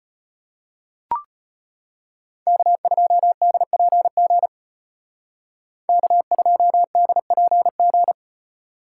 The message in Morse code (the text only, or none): K2DPG K2DPG